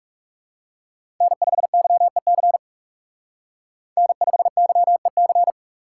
Morse code, heard as a keyed tone, about 37 wpm, 700 Hz. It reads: N5YEC N5YEC
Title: Speed Racing - USA 1x3 Call Signs 30wpm